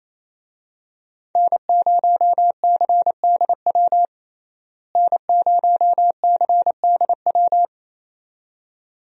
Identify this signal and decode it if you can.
Morse code, heard as a keyed tone, about 28 wpm, 700 Hz. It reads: N0CDW N0CDW
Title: USA Call Signs - 5 Characters - Repeated 2x - 28wpm